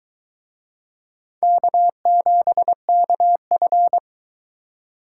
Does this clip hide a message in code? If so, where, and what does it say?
Yes: Morse code from 1.4 to 4.0 s, heard as a keyed tone, about 23 wpm, 700 Hz. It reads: K7KF